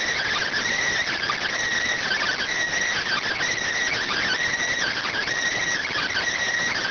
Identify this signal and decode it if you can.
Emergency Alert System (SAME) tones (header bursts), heard as harsh buzzing bursts. It is heard from 0.6 to 5.7 s.